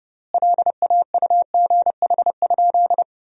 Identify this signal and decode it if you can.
Morse code, heard as a keyed tone, about 32 wpm, 700 Hz. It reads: LAUGH?
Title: 3x Word-Spacing - Sentences from Top 500 Words - 30wpm